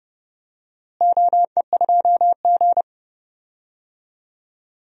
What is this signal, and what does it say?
Morse code, heard as a keyed tone, about 31 wpm, 700 Hz. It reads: OE2G
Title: DXCC-100 Call Signs - 4 Characters - Repeated 3x - 30wpm